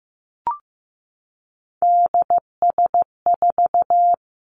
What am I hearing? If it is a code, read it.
Morse code, heard as a keyed tone, about 15 wpm, 700 Hz. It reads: DS4
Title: Three Challenging Letter-Number Combinations 15wpm